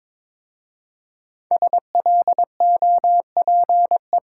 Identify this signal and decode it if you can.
Morse code, heard as a keyed tone, about 22 wpm, 700 Hz. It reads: SLOPE
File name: 5 Letter Words - First 3 Letters Given - 22wpm